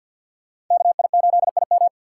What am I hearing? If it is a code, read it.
Morse code, heard as a keyed tone, about 52 wpm, 700 Hz. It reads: KI8IM
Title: USA Call Signs - 3x Character Spacing - 50wpm